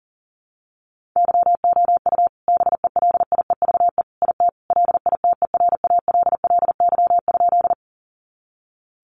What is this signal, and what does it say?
Morse code, heard as a keyed tone, about 40 wpm, 700 Hz. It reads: YOU BELIEVE IT LITERALLY?
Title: Sentences from Top 1400 Words 40wpm